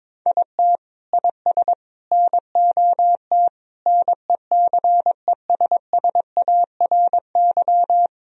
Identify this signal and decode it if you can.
Morse code, heard as a keyed tone, about 22 wpm, 700 Hz. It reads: IT IS NOT NECESSARY
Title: Sentences from Top 1000 Words - Review Part 1 - 22wpm